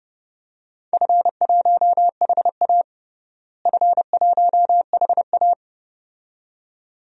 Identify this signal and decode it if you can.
Morse code, heard as a keyed tone, about 30 wpm, 700 Hz. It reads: F1HA F1HA
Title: DXCC-100 Call Signs - 4 Characters - Repeated 2x - 30wpm